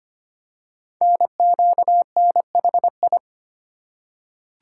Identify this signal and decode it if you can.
Morse code, heard as a keyed tone, about 26 wpm, 700 Hz. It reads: NQNHI
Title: Code Groups - 5 Letters - 25wpm